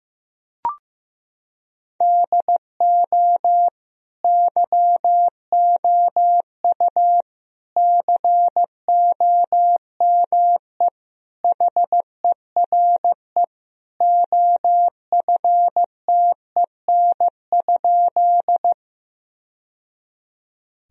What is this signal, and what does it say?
Morse code, heard as a keyed tone, about 15 wpm, 700 Hz. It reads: DO YOU COME HERE OFTEN?
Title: Sentences from Top 1000 Words - Review Part 1 - 15wpm